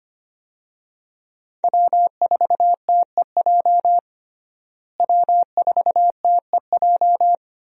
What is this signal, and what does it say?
Morse code, heard as a keyed tone, about 25 wpm, 700 Hz. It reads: W4TEJ W4TEJ
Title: USA Call Signs - 5 Characters - Repeated 3x - 25wpm